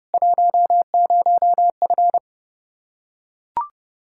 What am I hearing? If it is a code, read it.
Morse code, heard as a keyed tone, about 32 wpm, 700 Hz. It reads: 10F
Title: Practice QSO - Exchange 2 - Info - 30wpm